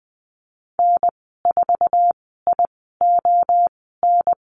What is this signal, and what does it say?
Morse code, heard as a keyed tone, about 20 wpm, spaced as for 15 wpm, 700 Hz. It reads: N4ION
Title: 048 Lesson 9g - Number 4 - Call Signs 20wpm